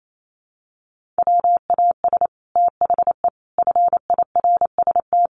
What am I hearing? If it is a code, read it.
Morse code, heard as a keyed tone, about 28 wpm, 700 Hz. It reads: WAS THE FIRST
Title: Sets of 4 Words 28wpm